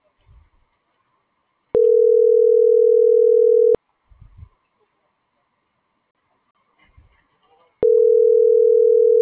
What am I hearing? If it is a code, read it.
Ringback tone (North America), 440 + 480 Hz.
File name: CENTRAL GENERAL / out alexia.jauregui